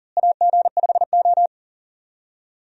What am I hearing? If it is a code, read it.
Morse code, heard as a keyed tone, about 40 wpm, 700 Hz. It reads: AG5O
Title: USA Call Signs - 4 Characters - Repeated 2x - 40wpm